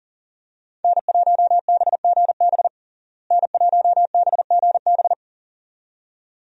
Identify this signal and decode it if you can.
Morse code, heard as a keyed tone, about 40 wpm, 700 Hz. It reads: N1BGB N1BGB
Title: USA Call Signs - 5 Characters - Repeated 2x - 40wpm